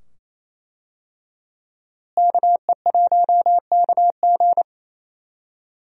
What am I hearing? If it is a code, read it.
Morse code, heard as a keyed tone, about 28 wpm, 700 Hz. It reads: KE1KG
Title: USA Call Signs - 5 Characters - Repeated 2x - 28wpm